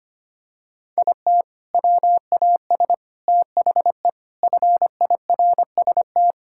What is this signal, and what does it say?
Morse code, heard as a keyed tone, about 25 wpm, 700 Hz. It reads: IT WAS THE FIRST
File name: practice / 25wpm / Sentences with 4-Word Phrases - 25wpm